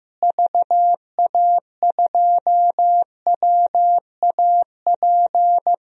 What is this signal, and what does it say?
Morse code, heard as a keyed tone, about 15 wpm, 700 Hz. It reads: VA2WAP
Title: DXCC-100 Call Signs - 6 Characters - Repeated 2x - 15wpm